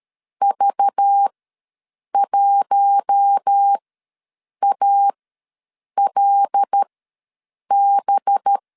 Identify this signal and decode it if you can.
Morse code, heard as a keyed tone, about 13 wpm, spaced as for 7 wpm, 800 Hz. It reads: V1ALB
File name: LEZIONI / 5 LBVC / 5 LBV ws 12wpm 3015